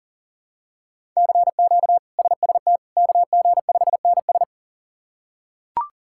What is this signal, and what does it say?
Morse code, heard as a keyed tone, about 40 wpm, 700 Hz. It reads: CQ SST KG5NS
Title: K1USN SST QSO Practice - 40wpm